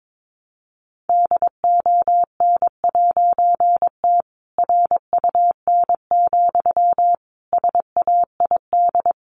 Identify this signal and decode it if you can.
Morse code, heard as a keyed tone, about 22 wpm, 700 Hz. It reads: DON'T RUN, SAID
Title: A Sound Of Thunder 22wpm